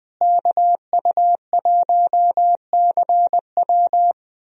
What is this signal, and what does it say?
Morse code, heard as a keyed tone, about 20 wpm, 700 Hz. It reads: KU1CW
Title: USA Call Signs - 5 Characters - Repeated 3x - 20wpm